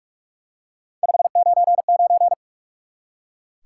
Morse code, heard as a keyed tone, about 46 wpm, 700 Hz. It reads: 599
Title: Top 90 QSO Elements 45wpm